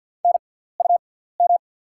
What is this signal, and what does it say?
Morse code, heard as a keyed tone, about 48 wpm, spaced as for 17 wpm, 690 Hz. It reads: NUM
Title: 6x Character Spacing - Top 100 Words - 50wpm